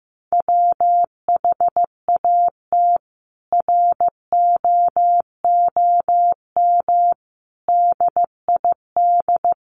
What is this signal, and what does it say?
Morse code, heard as a keyed tone, about 14 wpm, 700 Hz. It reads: WHAT ROOM DID
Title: Sentences from Top 600 Words 15wpm